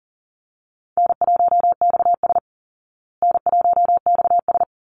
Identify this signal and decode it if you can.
Morse code, heard as a keyed tone, about 40 wpm, 700 Hz. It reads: N1XS N1XS
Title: Easy Call Signs Repeated 2x - 40wpm